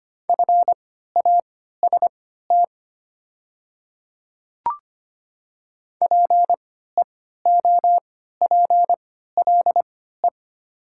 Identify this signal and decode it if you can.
Morse code, heard as a keyed tone, about 25 wpm, spaced as for 14 wpm, 700 Hz. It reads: FAST PEOPLE
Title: 3x Character Spacing - Top 500 Words - 25wpm